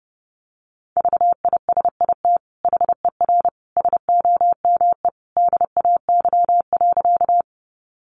Morse code, heard as a keyed tone, about 30 wpm, 700 Hz. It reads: VISIT HER SOME DAY.
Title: Sentences from Top 900 Words 30wpm